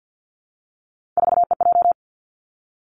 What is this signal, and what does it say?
Morse code, heard as a keyed tone, about 50 wpm, 700 Hz. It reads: 4EJ
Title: Three Challenging Letter-Number Combinations 50wpm